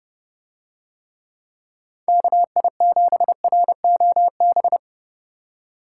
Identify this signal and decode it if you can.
Morse code, heard as a keyed tone, about 30 wpm, 700 Hz. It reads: KI7ROB